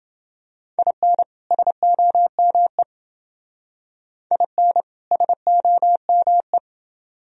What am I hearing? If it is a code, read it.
Morse code, heard as a keyed tone, about 28 wpm, 700 Hz. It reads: IN SOME IN SOME